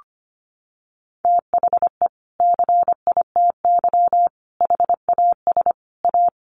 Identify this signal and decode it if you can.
Morse code, heard as a keyed tone, about 25 wpm, 700 Hz. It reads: THE CITY HAS A
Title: Sentences from Top 1300 Words 25wpm